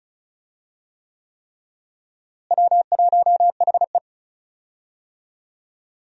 Morse code, heard as a keyed tone, about 36 wpm, 700 Hz. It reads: W1HE
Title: USA Call Signs - 4 Characters - Repeated 3x - 35wpm